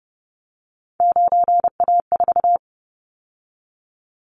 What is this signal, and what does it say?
Morse code, heard as a keyed tone, about 30 wpm, 700 Hz. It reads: 9A4
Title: Callsign Prefixes for Top 25 DXCC - 30wpm